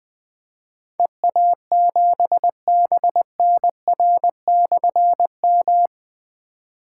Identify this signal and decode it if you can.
Morse code, heard as a keyed tone, about 20 wpm, 700 Hz. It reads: EA7BNR/M
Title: Difficult Call Signs 20wpm